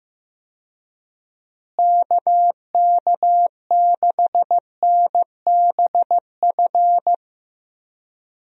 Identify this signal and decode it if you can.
Morse code, heard as a keyed tone, about 15 wpm, 700 Hz. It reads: KK6NBF
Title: USA Call Signs - 6 Characters - Repeated 2x - 15wpm